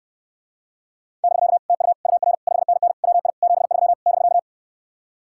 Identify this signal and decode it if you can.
Morse code, heard as a keyed tone, about 68 wpm, 700 Hz. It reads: <BK> TU UR 5NN WI 73 <BK>